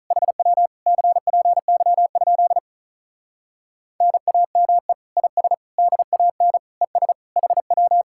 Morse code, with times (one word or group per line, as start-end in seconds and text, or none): 0.1-0.7 HW
0.9-2.6 CPY?
4.0-4.9 NAME
5.2-5.5 IS
5.8-6.6 DAN
6.8-7.1 ES
7.4-8.0 HW